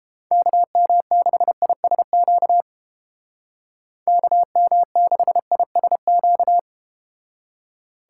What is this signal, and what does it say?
Morse code, heard as a keyed tone, about 31 wpm, 700 Hz. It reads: KM6ISQ KM6ISQ